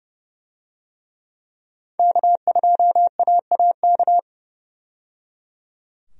Morse code, heard as a keyed tone, about 29 wpm, 700 Hz. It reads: K2AAK
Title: USA Call Signs - 5 Characters - 30wpm